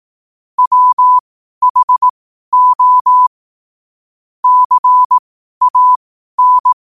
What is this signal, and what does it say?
Morse code, heard as a keyed tone, about 18 wpm, spaced as for 12 wpm, 1000 Hz. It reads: WHO CAN
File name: qotdfiles / 12wpm